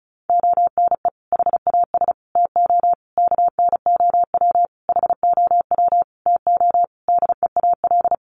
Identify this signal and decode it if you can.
Morse code, heard as a keyed tone, about 34 wpm, 700 Hz. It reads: ONE HAS TO KNOW HOW TO DEAL
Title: Sentences from Top 800 Words 35wpm